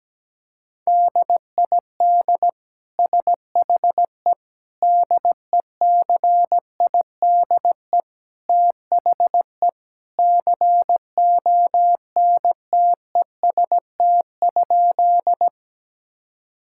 Morse code, heard as a keyed tone, about 17 wpm, 700 Hz. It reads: DID SHE DECIDE THE CONTEST?
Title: Sentences from Top 1600 Words 17wpm